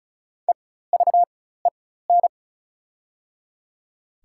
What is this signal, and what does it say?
Morse code, heard as a keyed tone, about 34 wpm, spaced as for 16 wpm, 700 Hz. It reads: EVEN
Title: 4x Character Spacing - Top 300 Words - 35wpm